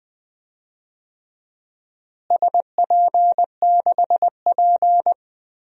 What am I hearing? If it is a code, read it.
Morse code, heard as a keyed tone, about 20 wpm, 700 Hz. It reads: SP6P
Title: DXCC-100 Call Signs - 4 Characters - Repeated 2x - 20wpm